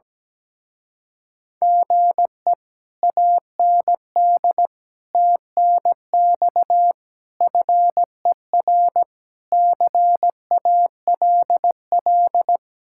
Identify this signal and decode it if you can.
Morse code, heard as a keyed tone, about 17 wpm, 700 Hz. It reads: GE AND TNX FER CALL
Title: Speed-Racing Common QSO Phrases Encore - 17wpm